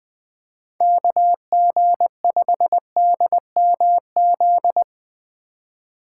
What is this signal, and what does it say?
Morse code, heard as a keyed tone, about 20 wpm, 700 Hz. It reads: KG5DMZ
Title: USA Call Signs - 6 Characters - 3x Character Spacing - 20wpm